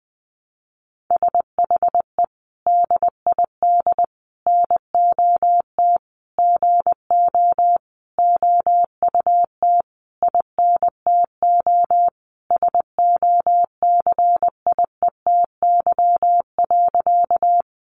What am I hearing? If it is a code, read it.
Morse code, heard as a keyed tone, about 20 wpm, 700 Hz. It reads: SHE DID NOT GO OUT INTO SOCIETY.
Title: Sentences from Top 700 Words 20wpm